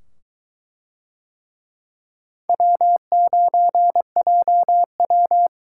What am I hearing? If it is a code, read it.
Morse code, heard as a keyed tone, about 22 wpm, 700 Hz. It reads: W9JW